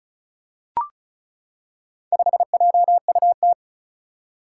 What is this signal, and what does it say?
Morse code, heard as a keyed tone, about 36 wpm, 700 Hz. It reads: 5JUT
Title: Four Challenging Letter-Number Combinations 35wpm